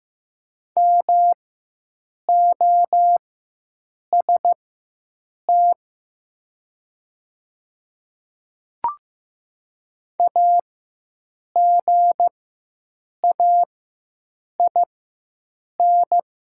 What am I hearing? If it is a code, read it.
Morse code, heard as a keyed tone, about 15 wpm, spaced as for 7 wpm, 700 Hz. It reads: MOST AGAIN
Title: 4x Character Spacing - Top 300 Words - 15wpm